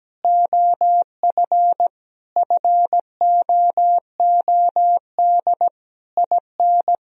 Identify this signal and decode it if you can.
Morse code, heard as a keyed tone, about 17 wpm, 700 Hz. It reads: OF FOOD IN
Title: Sentences from Top 1000 Words - Review Part 4 - 17wpm